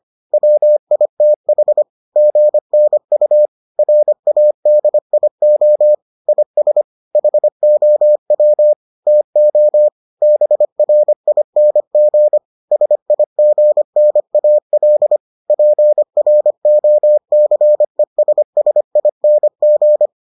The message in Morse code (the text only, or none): WITH GNU RADIO IS HOW TO BRING SIGNAL PROCESSING